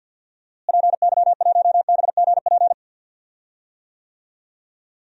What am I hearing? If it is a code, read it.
Morse code, heard as a keyed tone, about 46 wpm, 700 Hz. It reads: PY1BGP
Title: DXCC-100 Call Signs - 6 Characters - Repeated 3x - 50wpm